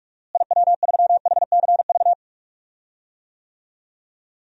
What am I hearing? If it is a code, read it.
Morse code, heard as a keyed tone, about 46 wpm, 700 Hz. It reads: IW3HCV
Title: DXCC-100 Call Signs - 6 Characters - Repeated 2x - 45wpm